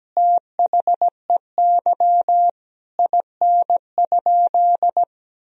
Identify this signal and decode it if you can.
Morse code, heard as a keyed tone, about 17 wpm, 700 Hz. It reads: THEY IN?
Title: Sentences from Top 1000 Words - Review Part 2 - 17wpm